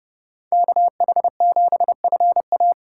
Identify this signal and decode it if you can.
Morse code, heard as a keyed tone, about 30 wpm, 700 Hz. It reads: KH7FA
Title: USA Call Signs - 5 Characters - 4x Character Spacing - 30wpm